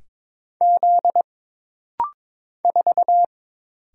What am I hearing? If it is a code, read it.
Morse code, heard as a keyed tone, about 22 wpm, 700 Hz. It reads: Z 4
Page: This rapid-fire timing is geared to push your ICR (Instant Character Recognition) skills.